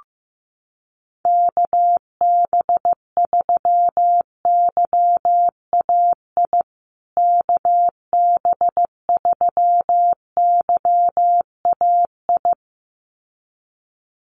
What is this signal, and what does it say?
Morse code, heard as a keyed tone, about 15 wpm, 700 Hz. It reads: KB3YAI KB3YAI
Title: USA Call Signs Repeated 2x 15wpm